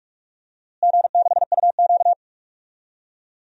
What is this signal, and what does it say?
Morse code, heard as a keyed tone, about 43 wpm, 700 Hz. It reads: G6UQ
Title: DXCC-100 Call Signs - 4 Characters - Repeated 3x - 45wpm